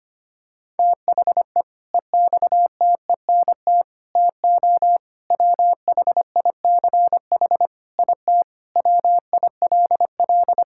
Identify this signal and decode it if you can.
Morse code, heard as a keyed tone, about 25 wpm, 700 Hz. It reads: THE EXTENT TO WHICH IT WILL
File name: practice / 25wpm / Sentences with 4-Word Phrases - 25wpm